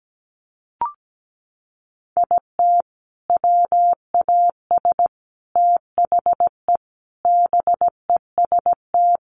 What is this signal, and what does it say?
Morse code, heard as a keyed tone, about 17 wpm, 700 Hz. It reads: IT WAS THE BEST
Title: Sentences from Top 1300 Words 17wpm